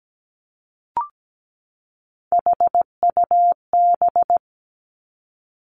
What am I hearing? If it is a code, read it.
Morse code, heard as a keyed tone, about 17 wpm, 700 Hz. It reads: HUB
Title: Three Challenging Letter-Number Combinations 17wpm